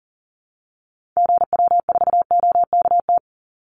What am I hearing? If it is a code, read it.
Morse code, heard as a keyed tone, about 40 wpm, 700 Hz. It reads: GW4OKT
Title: Call Signs - 40wpm